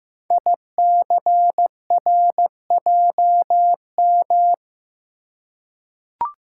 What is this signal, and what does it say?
Morse code, heard as a keyed tone, about 15 wpm, 700 Hz. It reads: ICRJM
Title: Code Groups - 5 Letters - 15wpm